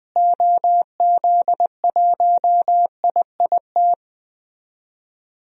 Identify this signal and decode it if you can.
Morse code, heard as a keyed tone, about 20 wpm, 700 Hz. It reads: OZ1IIT
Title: DXCC-100 Call Signs - 6 Characters - Repeated 3x - 20wpm